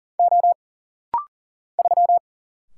Morse code, heard as a keyed tone, about 40 wpm, 700 Hz. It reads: O 3
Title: Single Letter-Number - Rapid-Fire 40wpm